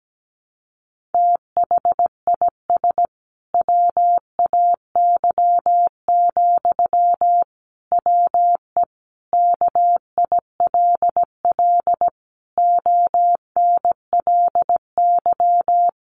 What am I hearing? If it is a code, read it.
Morse code, heard as a keyed tone, about 17 wpm, 700 Hz. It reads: THIS WAY, WE KILL ONLY
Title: A Sound Of Thunder 17wpm